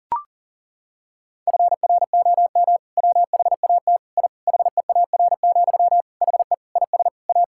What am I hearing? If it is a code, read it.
Morse code, heard as a keyed tone, about 40 wpm, 700 Hz. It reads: FROM WHAT I HEAR, HE IS A
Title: Sentences from Top 1000 Words - Review Part 6 - 40wpm